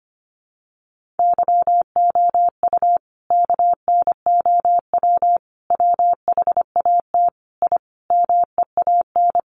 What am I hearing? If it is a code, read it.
Morse code, heard as a keyed tone, about 25 wpm, 700 Hz. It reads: YOU KNOW WHAT I MEAN
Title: Sets of 5 Words 25wpm